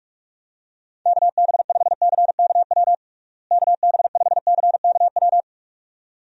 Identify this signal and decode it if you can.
Morse code, heard as a keyed tone, about 42 wpm, 700 Hz. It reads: KB5CKW KB5CKW
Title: USA Call Signs - 6 Characters - Repeated 2x - 45wpm